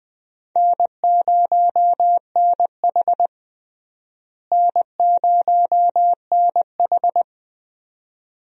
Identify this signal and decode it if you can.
Morse code, heard as a keyed tone, about 20 wpm, 700 Hz. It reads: N0NH N0NH
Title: USA Call Signs - 4 Characters - Repeated 2x - 20wpm